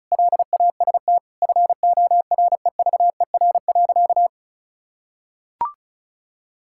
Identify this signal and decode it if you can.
Morse code, heard as a keyed tone, about 34 wpm, 700 Hz. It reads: LAST FOREVER.
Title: Sentences from Top 1600 Words 35wpm